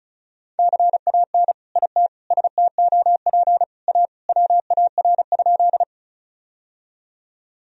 Morse code, heard as a keyed tone, about 36 wpm, 700 Hz. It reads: CAN IT STOP A WAR?
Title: Sentences from Top 1000 Words - Review Part 2 - 35wpm